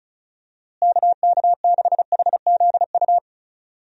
Morse code, heard as a keyed tone, about 34 wpm, 700 Hz. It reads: KK6HZU
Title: USA Call Signs - 6 Characters - Repeated 3x - 35wpm